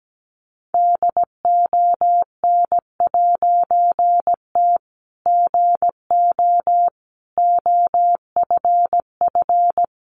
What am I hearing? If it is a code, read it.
Morse code, heard as a keyed tone, about 17 wpm, 700 Hz. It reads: DON'T GO OFF
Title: A Sound Of Thunder 17wpm